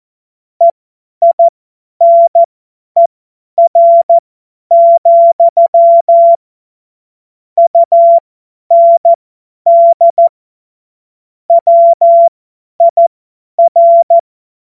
Morse code, heard as a keyed tone, about 14 wpm, spaced as for 10 wpm, 680 Hz. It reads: EINER, UND WIR
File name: Morsekurs / Witze / Witz 21. Juni 2022 50 F70